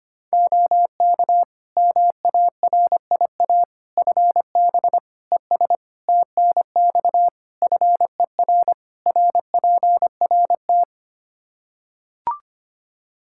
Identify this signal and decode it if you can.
Morse code, heard as a keyed tone, about 25 wpm, 700 Hz. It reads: OK MARIA FB ES TNX FER RPRT
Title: Practice QSO - Exchange 2 - Info - 25wpm